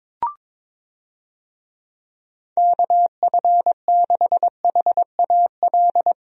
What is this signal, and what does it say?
Morse code, heard as a keyed tone, about 22 wpm, 700 Hz. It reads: KF6HAL
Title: USA Call Signs - 6 Characters - 22wpm